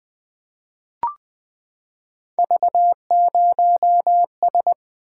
Morse code, heard as a keyed tone, about 20 wpm, 700 Hz. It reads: V0S
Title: Three Letter-Number Combinations - 20wpm